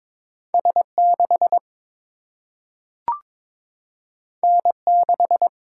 Morse code, heard as a keyed tone, about 22 wpm, 700 Hz. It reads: S6 N6
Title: Single-Letter Single-Number 22wpm